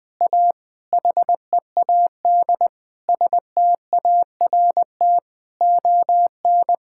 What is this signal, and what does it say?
Morse code, heard as a keyed tone, about 20 wpm, 700 Hz. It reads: A HEAD START ON
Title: Sentences with 9-Word Phrases - 20wpm